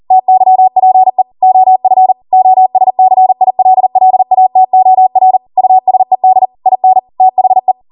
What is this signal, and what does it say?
Morse code, heard as a keyed tone, about 40 wpm, 750 Hz. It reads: TYPE OF OSCILLATOR USED IN THE